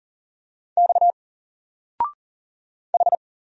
Morse code, heard as a keyed tone, about 40 wpm, 700 Hz. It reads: X H
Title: Single Letter-Number 40wpm